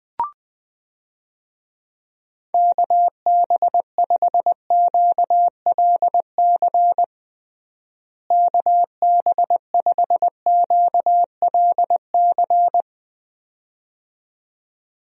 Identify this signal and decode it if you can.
Morse code, heard as a keyed tone, about 20 wpm, 700 Hz. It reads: KB5QLC KB5QLC
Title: USA Call Signs - 6 Characters - Repeated 2x - 20wpm